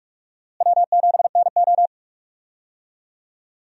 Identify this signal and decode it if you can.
Morse code, heard as a keyed tone, about 45 wpm, 700 Hz. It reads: W7NO
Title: USA Call Signs - 4 Characters - Repeated 2x - 45wpm